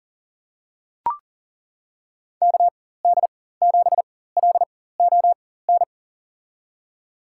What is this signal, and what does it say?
Morse code, heard as a keyed tone, about 40 wpm, spaced as for 19 wpm, 700 Hz. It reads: KD7LON
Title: USA Call Signs - 6 Characters - 4x Character Spacing - 40wpm